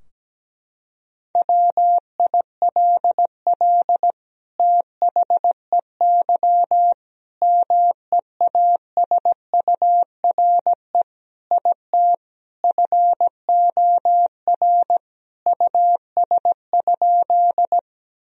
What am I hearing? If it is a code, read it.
Morse code, heard as a keyed tone, about 17 wpm, 700 Hz. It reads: WILL THEY MEASURE IT FOR US?
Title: Sentences from Top 1000 Words - Review Part 2 - 17wpm